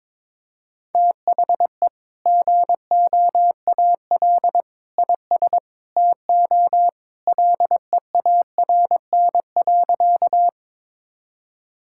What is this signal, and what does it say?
Morse code, heard as a keyed tone, about 22 wpm, 700 Hz. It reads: THE GOAL IS TO LEARN.
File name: practice / 22wpm / Sentences with 2-Word Phrases - 22wpm